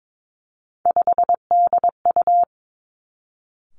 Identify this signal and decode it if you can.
Morse code, heard as a keyed tone, about 22 wpm, 700 Hz. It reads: 5DU
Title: Three Challenging Letter-Number Combinations 22wpm